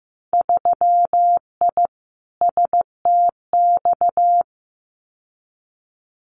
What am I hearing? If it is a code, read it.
Morse code, heard as a keyed tone, about 15 wpm, 700 Hz. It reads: 3I STX
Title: Winter Field Day Exchanges 15wpm